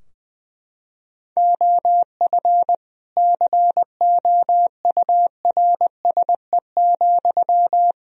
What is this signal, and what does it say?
Morse code, heard as a keyed tone, about 20 wpm, 700 Hz. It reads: OF COURSE,
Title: Sentences from Top 1000 Words - Review Part 3 - 20wpm